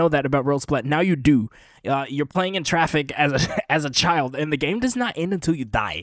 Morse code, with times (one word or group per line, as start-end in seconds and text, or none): none